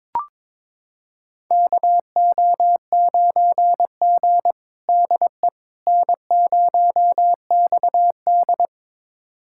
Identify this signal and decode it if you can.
Morse code, heard as a keyed tone, about 22 wpm, 700 Hz. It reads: KO9G DE N0XD
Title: USA Callsign DE Callsign - 4 Characters - 22wpm